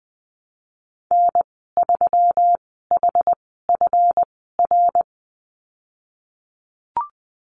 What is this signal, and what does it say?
Morse code, heard as a keyed tone, about 20 wpm, spaced as for 15 wpm, 700 Hz. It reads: N3HFR
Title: 161 Lesson 26g - Letter g - Call Signs 20wpm